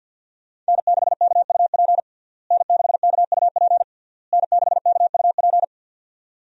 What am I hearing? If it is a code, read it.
Morse code, heard as a keyed tone, about 52 wpm, 700 Hz. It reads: N6KUP N6KUP N6KUP
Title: USA Call Signs - 5 Characters - Repeated 3x - 50wpm